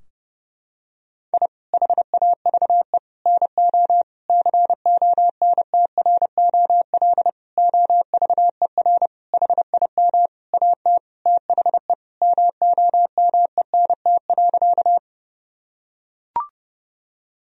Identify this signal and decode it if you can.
Morse code, heard as a keyed tone, about 32 wpm, 700 Hz. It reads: I HAVE NO CONTROL OVER HIM AT THE MOMENT.
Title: Sentences from Top 1000 Words - Review Part 3 - 30wpm